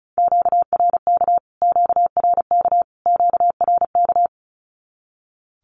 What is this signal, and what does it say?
Morse code, heard as a keyed tone, about 34 wpm, 700 Hz. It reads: QRK QRK QRK
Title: Q-Codes Repeated 3x 35wpm